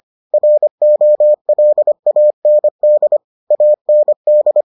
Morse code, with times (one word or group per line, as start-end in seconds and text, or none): 0.3-3.2 ROLAND
3.5-4.6 AND